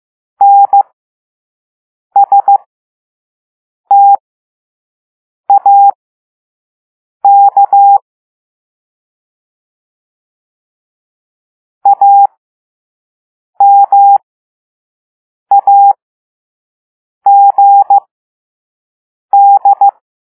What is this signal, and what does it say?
Morse code, heard as a keyed tone, about 15 wpm, spaced as for 5 wpm, 800 Hz. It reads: NSTAK AMAGD